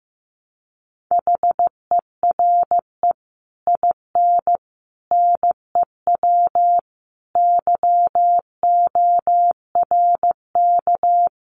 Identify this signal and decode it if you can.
Morse code, heard as a keyed tone, about 14 wpm, 700 Hz. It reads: HERE IN NEW YORK
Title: Sets of 5 Words Encore2 15wpm